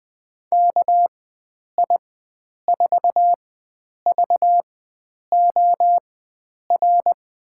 Morse code, heard as a keyed tone, about 20 wpm, spaced as for 9 wpm, 700 Hz. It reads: KI4VOR
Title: USA Call Signs - 6 Characters - 4x Character Spacing - 20wpm